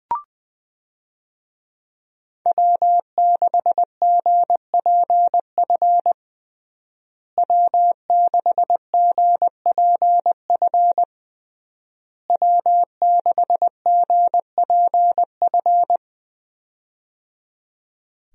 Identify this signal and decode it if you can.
Morse code, heard as a keyed tone, about 21 wpm, 700 Hz. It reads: W6GPF W6GPF W6GPF
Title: USA Call Signs - 5 Characters - Repeated 3x - 20wpm